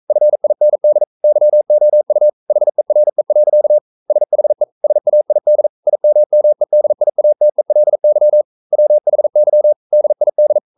Morse code, heard as a keyed tone, about 42 wpm, 600 Hz. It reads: FIND YOU HERE. SHE SAID IMMEDIATELY WHY DID
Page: Romanen Never af Ken Follet på engelsk.